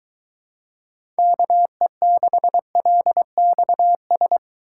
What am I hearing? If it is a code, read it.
Morse code, heard as a keyed tone, about 23 wpm, 700 Hz. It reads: KE6LXS